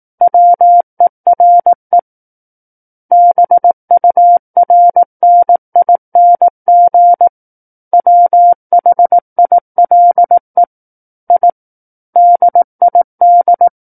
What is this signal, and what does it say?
Morse code, heard as a keyed tone, about 18 wpm, 700 Hz. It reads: WERE BURNING WHILE I DID
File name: Book One / 09-Chapter 9 - 18 WPM